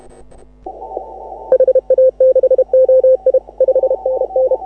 Morse code, heard as a keyed tone, about 32 wpm, 530 Hz. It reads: HA6OI 5NN
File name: AUDIO2018 / HA6OI CW